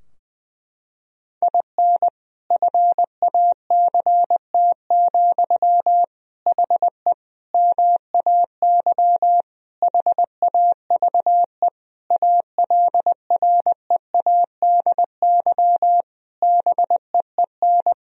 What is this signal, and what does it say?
Morse code, heard as a keyed tone, about 20 wpm, 700 Hz. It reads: IN FACT, HE MAY HAVE ALREADY BEEN
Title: Sentences from Top 1000 Words - Review Part 3 - 20wpm